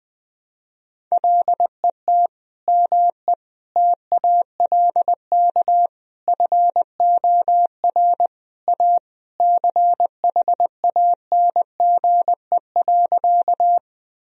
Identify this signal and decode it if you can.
Morse code, heard as a keyed tone, about 20 wpm, 700 Hz. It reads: LET ME TALK FOR A CHANGE.
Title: Sentences from Top 1000 Words - Review Part 1 - 20wpm